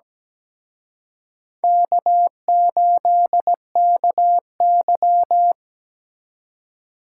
Morse code, heard as a keyed tone, about 20 wpm, 700 Hz. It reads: K8KY
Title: Speed Racing - USA 1x2 Call Signs 17wpm